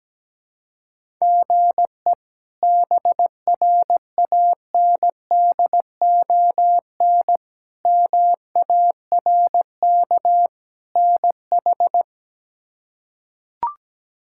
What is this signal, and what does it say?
Morse code, heard as a keyed tone, about 17 wpm, 700 Hz. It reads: GE BRANDON MARK NH
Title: K1USN SST QSO Practice - 17wpm